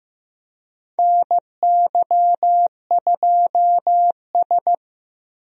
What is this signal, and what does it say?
Morse code, heard as a keyed tone, about 16 wpm, 700 Hz. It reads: NY2S
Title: USA Call Signs - 4 Characters - Repeated 2x - 15wpm